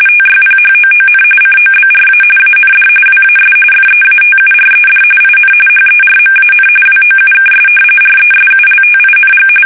RTTY, about 45.45 baud. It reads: TYPHOON COBRA AND WAS FEATURED IN HERMAN WOUKS RESULTING
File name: fsk850 / battleshipnj